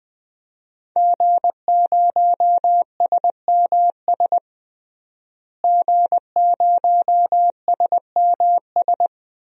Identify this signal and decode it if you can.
Morse code, heard as a keyed tone, about 20 wpm, 700 Hz. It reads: G0SMS G0SMS
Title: DXCC-100 Call Signs - 5 Characters - Repeated 2x - 20wpm